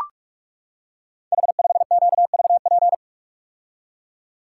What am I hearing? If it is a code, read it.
Morse code, heard as a keyed tone, about 44 wpm, 700 Hz. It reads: H5QVP
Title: Code Groups - 5 Letter:Numbers - 45wpm